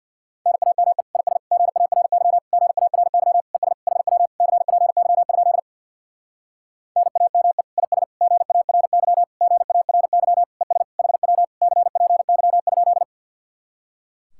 Morse code, heard as a keyed tone, about 61 wpm, 700 Hz. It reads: NAME IS GARY GARY ES HW CPY? NAME IS GARY GARY ES HW CPY?
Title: Speed-Racing Common QSO Phrases - 50wpm